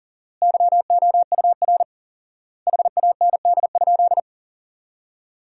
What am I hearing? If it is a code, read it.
Morse code, heard as a keyed tone, about 40 wpm, 700 Hz. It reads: YOUR HAND?
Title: 4x Word-Spacing - Sentences from Top 200 Words - 40wpm